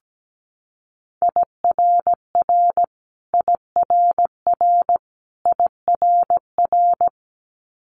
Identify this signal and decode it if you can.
Morse code, heard as a keyed tone, about 17 wpm, 700 Hz. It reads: IRR IRR IRR
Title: Multi-letter Phonemes Repeated 3x 17wpm